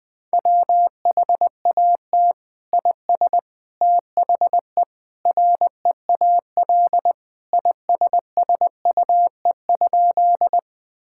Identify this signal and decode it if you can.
Morse code, heard as a keyed tone, about 21 wpm, 700 Hz. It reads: WHAT IS THE REAL ISSUE?
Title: Sentences from Top 1000 Words - Review Part 6 - 20wpm